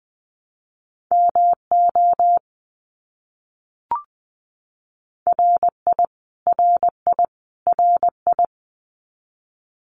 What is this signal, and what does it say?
Morse code, heard as a keyed tone, about 20 wpm, 700 Hz. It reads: MO RI RI RI
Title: US States Repeated 3x 20wpm